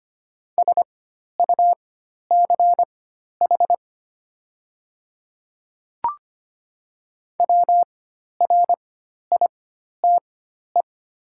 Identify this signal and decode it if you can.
Morse code, heard as a keyed tone, about 25 wpm, spaced as for 12 wpm, 700 Hz. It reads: SUCH WRITE
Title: 4x Character Spacing - Top 200 Words - 25wpm